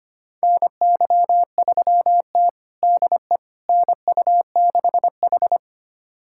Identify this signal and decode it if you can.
Morse code, heard as a keyed tone, about 25 wpm, 700 Hz. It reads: NY3T DE NU6H
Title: USA Callsign DE Callsign - 4 Characters - 25wpm